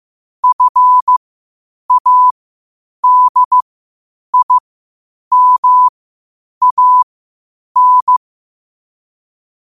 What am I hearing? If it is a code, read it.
Morse code, heard as a keyed tone, about 15 wpm, spaced as for 8 wpm, 1000 Hz. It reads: FADIMAN
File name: qotdfiles / 8wpm